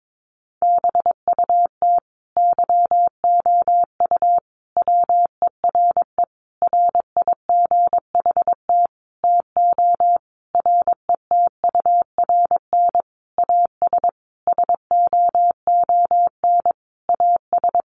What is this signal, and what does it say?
Morse code, heard as a keyed tone, about 22 wpm, 700 Hz. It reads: BUT YOU WERE RIGHT TO RETURN AS SOON AS
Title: Sentences from Top 800 Words 22wpm